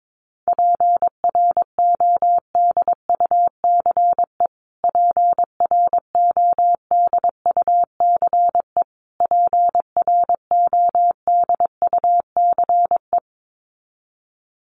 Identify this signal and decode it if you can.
Morse code, heard as a keyed tone, about 22 wpm, 700 Hz. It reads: PRODUCE PRODUCE PRODUCE
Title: Top 500 Words Repeated 22wpm